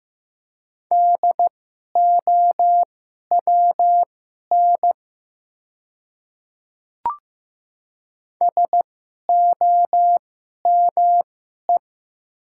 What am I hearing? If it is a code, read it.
Morse code, heard as a keyed tone, about 15 wpm, spaced as for 11 wpm, 700 Hz. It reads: DOWN SOME
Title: 2x Character Spacing - Top 100 Words - 15wpm